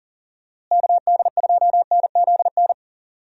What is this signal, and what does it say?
Morse code, heard as a keyed tone, about 40 wpm, 700 Hz. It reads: KD2NZN
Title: USA Call Signs - 6 Characters - Repeated 2x - 40wpm